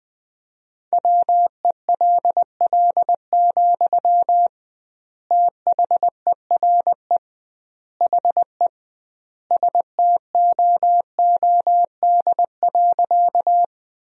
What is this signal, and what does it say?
Morse code, heard as a keyed tone, about 20 wpm, 700 Hz. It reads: WELL, THERE HE STOOD.
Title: 2x Word-Spacing - Sentences from Top 500 Words - 20wpm